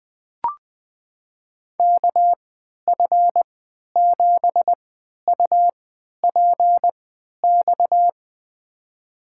Morse code, heard as a keyed tone, about 20 wpm, spaced as for 11 wpm, 700 Hz. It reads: KF7UPX
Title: USA Call Signs - 6 Characters - 3x Character Spacing - 20wpm